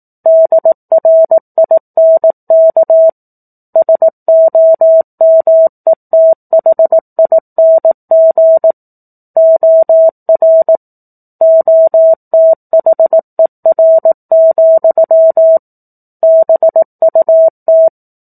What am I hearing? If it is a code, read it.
Morse code, heard as a keyed tone, about 18 wpm, 650 Hz. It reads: DRINK SOMETHING OR OTHER, BUT
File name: 18 WPM